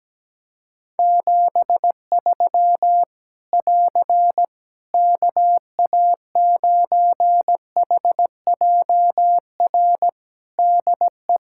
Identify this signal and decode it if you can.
Morse code, heard as a keyed tone, about 17 wpm, 700 Hz. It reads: 73 <AR> KA9HJR DE
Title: Practice QSO - Exchange 4 - Ending - 17wpm